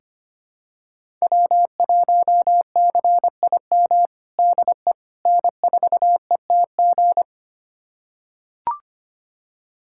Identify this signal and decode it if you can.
Morse code, heard as a keyed tone, about 25 wpm, 700 Hz. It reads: W1CIM DE N4ETG
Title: USA Callsign DE Callsign - 5 Characters - 25wpm